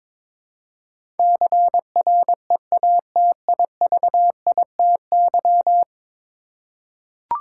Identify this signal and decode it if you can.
Morse code, heard as a keyed tone, about 22 wpm, 700 Hz. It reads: CREATIVITY
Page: morse-code-ninja-practice-22wpm